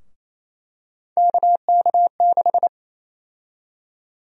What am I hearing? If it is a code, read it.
Morse code, heard as a keyed tone, about 28 wpm, 700 Hz. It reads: KK6
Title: USA Call Sign Prefixes - 28wpm